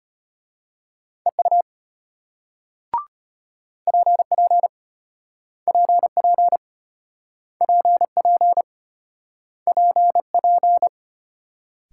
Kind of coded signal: Morse code

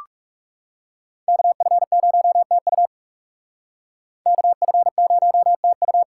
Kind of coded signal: Morse code

